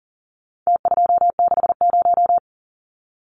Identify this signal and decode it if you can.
Morse code, heard as a keyed tone, about 40 wpm, 700 Hz. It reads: T260
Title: Numbers - Contest Style - 4-Digit 40wpm